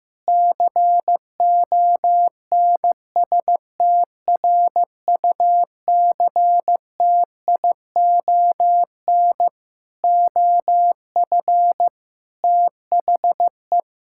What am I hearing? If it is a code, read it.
Morse code, heard as a keyed tone, about 15 wpm, 700 Hz. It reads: CONSTRUCTION OF THE
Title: Sentences from Top 1600 Words 15wpm